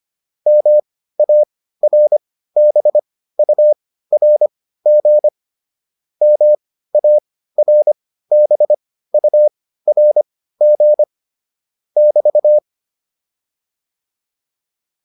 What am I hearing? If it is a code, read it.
Morse code, heard as a keyed tone, about 25 wpm, spaced as for 15 wpm, 600 Hz. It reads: MARBURG MARBURG <BT>
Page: intermediate-practice-files